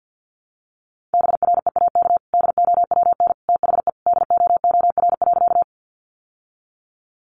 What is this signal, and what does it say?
Morse code, heard as a keyed tone, about 48 wpm, 700 Hz. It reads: BREAK DOWN THE DOOR.
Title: Sentences from Top 900 Words 50wpm